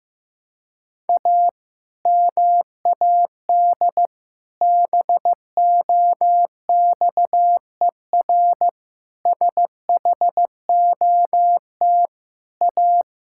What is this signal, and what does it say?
Morse code, heard as a keyed tone, about 16 wpm, 700 Hz. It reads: A MAD BOXER SHOT A
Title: Pangram Challenge v2 - 15wpm